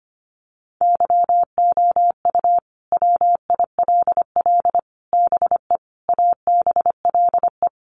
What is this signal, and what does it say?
Morse code, heard as a keyed tone, about 25 wpm, 700 Hz. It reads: YOU WILL BE ABLE
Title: Sets of 5 Words Encore2 25wpm